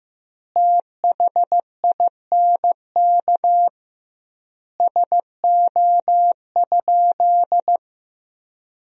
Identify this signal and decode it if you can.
Morse code, heard as a keyed tone, about 15 wpm, 700 Hz. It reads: THINK SO?
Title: 2x Word-Spacing - Sentences from Top 300 Words - 15wpm